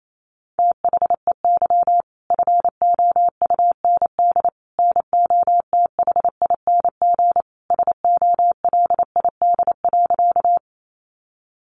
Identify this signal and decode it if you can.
Morse code, heard as a keyed tone, about 28 wpm, 700 Hz. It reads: THEY FOUND NOTHING SOLID.
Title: Sentences from Top 1500 Words 28wpm